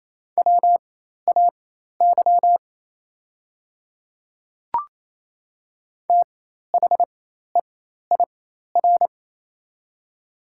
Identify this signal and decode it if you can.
Morse code, heard as a keyed tone, about 28 wpm, spaced as for 13 wpm, 690 Hz. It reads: WAY THEIR
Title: 4x Character Spacing - Top 200 Words - 28wpm